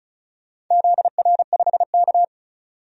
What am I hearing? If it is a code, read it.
Morse code, heard as a keyed tone, about 36 wpm, 700 Hz. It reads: ZR5K
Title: DXCC-100 Call Signs - 4 Characters - 35wpm